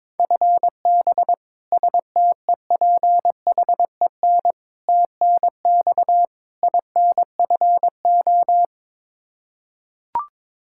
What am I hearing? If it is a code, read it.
Morse code, heard as a keyed tone, about 22 wpm, 700 Hz. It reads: FB STEPHEN TNX INFO
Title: Practice QSO - Exchange 3 - More Info - 22wpm